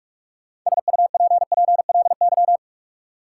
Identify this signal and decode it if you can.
Morse code, heard as a keyed tone, about 43 wpm, 700 Hz. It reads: SUPPLY
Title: Sentences from Top 1200 Words - New Word Drill Alt - 45wpm